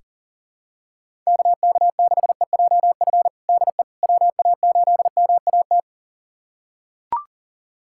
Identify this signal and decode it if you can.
Morse code, heard as a keyed tone, about 40 wpm, 700 Hz. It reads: KK6EJF DE WA8MAT
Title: USA Callsign DE Callsign 40wpm